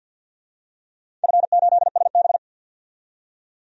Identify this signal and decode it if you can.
Morse code, heard as a keyed tone, about 46 wpm, 700 Hz. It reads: F8SB
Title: DXCC-100 Call Signs - 4 Characters - 50wpm